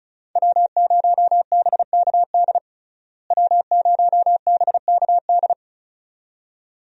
Morse code, heard as a keyed tone, about 36 wpm, 700 Hz. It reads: W0BKD W0BKD
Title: USA Call Signs - 5 Characters - Repeated 2x - 35wpm